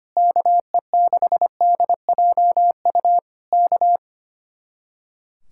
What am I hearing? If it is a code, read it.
Morse code, heard as a keyed tone, about 24 wpm, 700 Hz. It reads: KE6DJU K
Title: Speed-Racing Common QSO Phrases Encore - 25wpm